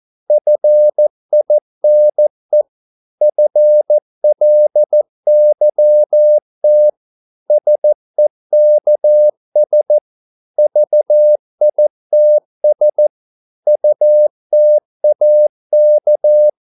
Morse code, heard as a keyed tone, about 14 wpm, 600 Hz. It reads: FINE FLYT SEKS VITS UTAK
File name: Korte ord DK 14wpm